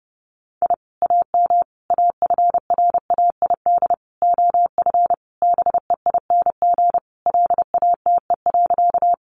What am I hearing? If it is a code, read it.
Morse code, heard as a keyed tone, about 28 wpm, 700 Hz. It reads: I AM AFRAID OF BEING LATE.
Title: Sentences from Top 1500 Words 30wpm